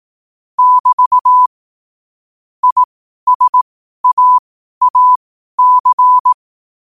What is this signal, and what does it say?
Morse code, heard as a keyed tone, about 18 wpm, spaced as for 12 wpm, 1000 Hz. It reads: <BT> ISAAC
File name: qotdfiles / 12wpm